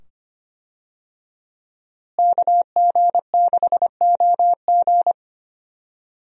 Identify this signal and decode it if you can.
Morse code, heard as a keyed tone, about 25 wpm, 700 Hz. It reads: KG6OG
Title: USA Call Signs - 5 Characters - Repeated 2x - 25wpm